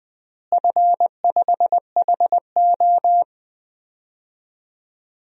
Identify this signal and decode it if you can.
Morse code, heard as a keyed tone, about 19 wpm, 700 Hz. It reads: F5HO
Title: DXCC-100 Call Signs - 4 Characters - Repeated 3x - 20wpm